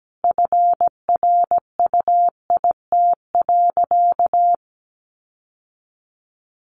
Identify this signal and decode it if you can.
Morse code, heard as a keyed tone, about 17 wpm, 700 Hz. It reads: FRUIT.
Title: Sentences from Top 1300 Words 17wpm